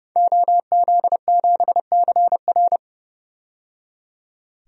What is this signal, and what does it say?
Morse code, heard as a keyed tone, about 30 wpm, 700 Hz. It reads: OZ7CR
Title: DXCC-100 Call Signs - 5 Characters - 30wpm